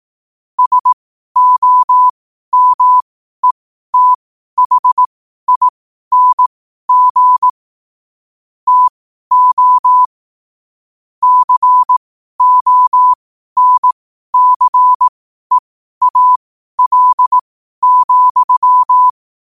Morse code, heard as a keyed tone, about 18 wpm, spaced as for 12 wpm, 1000 Hz. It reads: SOMETHING TO CONCEAL,
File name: qotdfiles / 12wpm